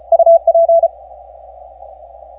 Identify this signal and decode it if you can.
Morse code, heard as a keyed tone, about 36 wpm, 670 Hz. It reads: UP
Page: 3Y0K 17 Meters Longpath 1400Z